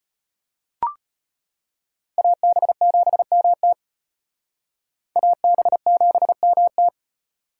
Morse code, heard as a keyed tone, about 33 wpm, 700 Hz. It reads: AB7MT AB7MT